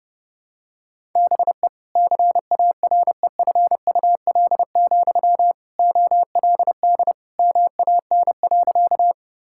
Morse code, heard as a keyed tone, about 28 wpm, 700 Hz. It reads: BE CAREFUL, OLD MAN.
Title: Sentences from Top 1600 Words 30wpm